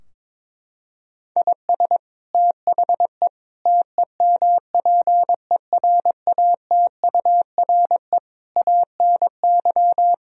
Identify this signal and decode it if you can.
Morse code, heard as a keyed tone, about 22 wpm, 700 Hz. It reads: IS THE TEMPERATURE ANY
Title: Sentences from Top 1000 Words - Review Part 4 - 22wpm